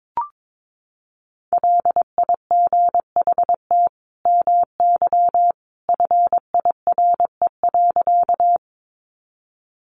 Morse code, heard as a keyed tone, about 22 wpm, 700 Hz. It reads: LIGHT MY FIRE.
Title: Sentences from Top 400 Words 22wpm